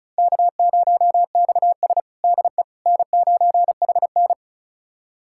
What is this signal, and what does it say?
Morse code, heard as a keyed tone, about 36 wpm, 700 Hz. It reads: K0XS DE N9HN
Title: USA Callsign DE Callsign - 4 Characters - 35wpm